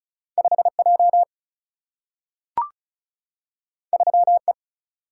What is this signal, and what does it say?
Morse code, heard as a keyed tone, about 36 wpm, 700 Hz. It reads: 5 J 3 E
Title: Single-Letter Single-Number 35wpm